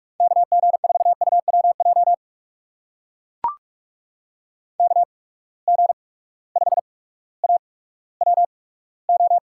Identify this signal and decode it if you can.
Morse code, heard as a keyed tone, about 44 wpm, spaced as for 19 wpm, 700 Hz. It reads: KG4UWJ KG5AWO
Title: USA Call Signs - 6 Characters - 8x Character Spacing - 45wpm